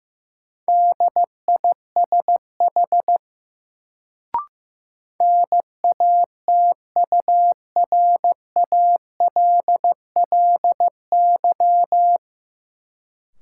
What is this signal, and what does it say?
Morse code, heard as a keyed tone, about 15 wpm, 700 Hz. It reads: DISH NATURALLY
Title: Sentences from Top 1300 Words - New Word Drill Alt - 15wpm